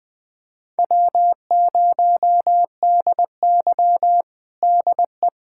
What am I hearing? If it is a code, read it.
Morse code, heard as a keyed tone, about 21 wpm, 700 Hz. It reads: W0DY DE
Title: USA Callsign DE Callsign - 4 Characters - 20wpm